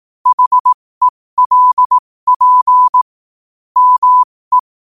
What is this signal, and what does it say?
Morse code, heard as a keyed tone, about 18 wpm, spaced as for 15 wpm, 1000 Hz. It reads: HELP ME
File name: qotdfiles / 15wpm